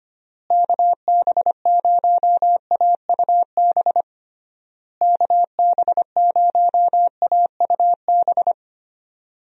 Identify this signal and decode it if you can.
Morse code, heard as a keyed tone, about 25 wpm, 700 Hz. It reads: KB0AUB KB0AUB
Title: USA Call Signs - 6 Characters - Repeated 3x - 25wpm